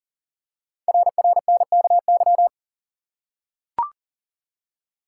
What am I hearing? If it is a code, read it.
Morse code, heard as a keyed tone, about 43 wpm, 700 Hz. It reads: RRNKY
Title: Code Groups - 5 Letters - 40wpm